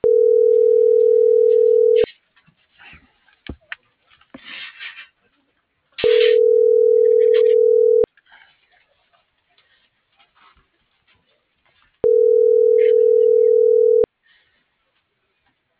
Ringback tone (North America), 440 + 480 Hz.